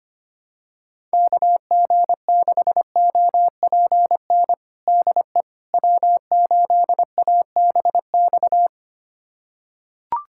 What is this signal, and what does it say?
Morse code, heard as a keyed tone, about 25 wpm, 700 Hz. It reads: KG6OPN DE W8ABX
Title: USA Callsign DE Callsign 25wpm